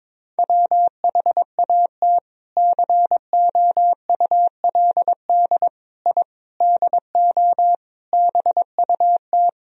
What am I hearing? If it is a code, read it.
Morse code, heard as a keyed tone, about 22 wpm, 700 Hz. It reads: WHAT COULD I DO BUT
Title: Sentences from Top 1000 Words - Review Part 1 - 22wpm